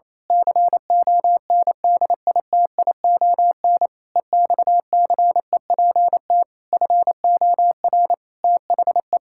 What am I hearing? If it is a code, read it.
Morse code, heard as a keyed tone, about 28 wpm, 700 Hz. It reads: CONDITION EXCEPT FOR THE
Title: Pangram Challenge v2 - 28wpm